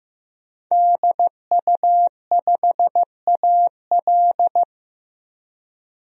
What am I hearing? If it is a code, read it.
Morse code, heard as a keyed tone, about 16 wpm, 700 Hz. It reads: DU5AL
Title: DXCC-100 Call Signs - 5 Characters - 15wpm